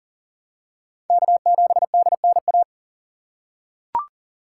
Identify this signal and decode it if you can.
Morse code, heard as a keyed tone, about 40 wpm, 700 Hz. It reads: K7DNA
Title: USA Call Signs - 5 Characters - 4x Character Spacing - 40wpm